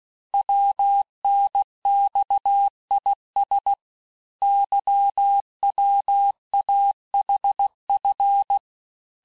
Morse code, heard as a keyed tone, about 16 wpm, 800 Hz. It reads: WNXIS YWAHF
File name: cours-du-lundi files / cours-1592--teste-chiffre-vitesse-15mpm